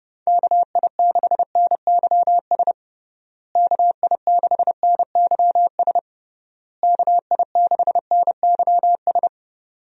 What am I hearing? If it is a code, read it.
Morse code, heard as a keyed tone, about 32 wpm, 700 Hz. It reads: KI6NYS KI6NYS KI6NYS
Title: USA Call Signs - 6 Characters - Repeated 3x - 30wpm